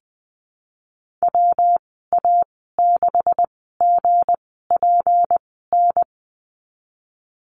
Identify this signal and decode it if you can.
Morse code, heard as a keyed tone, about 20 wpm, spaced as for 15 wpm, 700 Hz. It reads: WA6GPN
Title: 179 Lesson 29g - Slash - Call Signs 20wpm